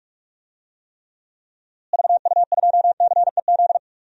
Morse code, heard as a keyed tone, about 43 wpm, 700 Hz. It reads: VU2CEZ
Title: DXCC-100 Call Signs - 6 Characters - Repeated 2x - 45wpm